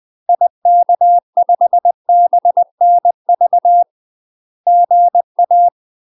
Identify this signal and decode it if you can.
Morse code, heard as a keyed tone, about 20 wpm, 700 Hz. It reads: IK5BNV GA
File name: UK MASTER QSOs TRAINING / 20WPM / UK MASTER QSOs 0067